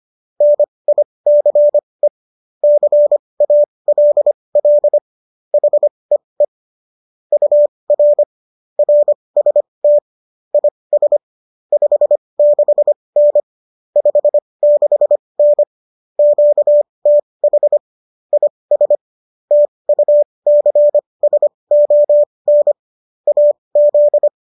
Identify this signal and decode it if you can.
Morse code, heard as a keyed tone, about 25 wpm, spaced as for 20 wpm, 590 Hz. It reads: NICE CALL HEE UR RST IS 56N 56N QTH IS TUCSON AZ